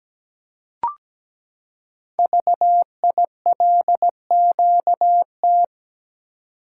Morse code, heard as a keyed tone, about 17 wpm, 700 Hz. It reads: VILQT
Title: Code Groups - 5 Letters - 17wpm